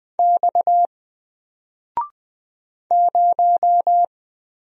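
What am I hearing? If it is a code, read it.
Morse code, heard as a keyed tone, about 20 wpm, 700 Hz. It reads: X 0
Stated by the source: Single letter and number Morse drill at 20 wpm, from N4LCD.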